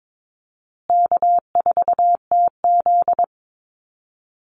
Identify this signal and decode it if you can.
Morse code, heard as a keyed tone, about 22 wpm, 700 Hz. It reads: K4TZ
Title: Easy Call Signs Repeated 3x - 22wpm